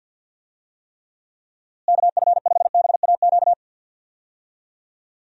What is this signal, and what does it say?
Morse code, heard as a keyed tone, about 50 wpm, 700 Hz. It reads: KF5BAQ
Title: USA Call Signs - 6 Characters - Repeated 2x - 50wpm